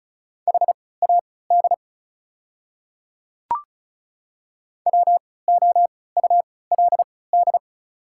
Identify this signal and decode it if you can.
Morse code, heard as a keyed tone, about 36 wpm, spaced as for 20 wpm, 700 Hz. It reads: HAD WOULD
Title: 3x Character Spacing - Top 300 Words - 35wpm